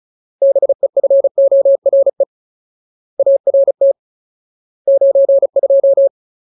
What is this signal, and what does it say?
Morse code, heard as a keyed tone, about 34 wpm, 550 Hz. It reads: BEFORE ART 92
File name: Ultimate Word List / 35WPM Ultimate